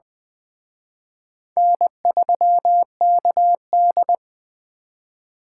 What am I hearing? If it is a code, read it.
Morse code, heard as a keyed tone, about 20 wpm, 700 Hz. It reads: N3KD